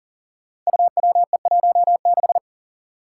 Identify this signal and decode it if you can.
Morse code, heard as a keyed tone, about 41 wpm, 700 Hz. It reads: UWE16
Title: Five Challenging Letter-Number Combinations 40wpm